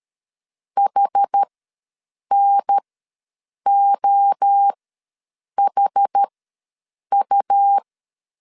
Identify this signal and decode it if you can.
Morse code, heard as a keyed tone, about 13 wpm, spaced as for 7 wpm, 800 Hz. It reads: HNOHU
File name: LEZIONI / 6 PKX0 / 6 P ws 12wpm 3117